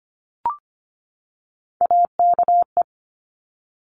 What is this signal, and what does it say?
Morse code, heard as a keyed tone, about 25 wpm, 700 Hz. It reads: AKE
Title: Top Three Letter Combinations in Words 25wpm